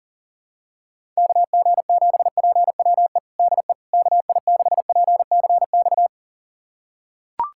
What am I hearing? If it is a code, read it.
Morse code, heard as a keyed tone, about 40 wpm, 700 Hz. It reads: KG7PWE DE KI6PCX
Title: USA Callsign DE Callsign - 6 Characters - 40wpm